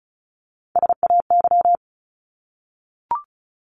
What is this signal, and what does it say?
Morse code, heard as a keyed tone, about 34 wpm, 700 Hz. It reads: SAY
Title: Top 200 Words 35wpm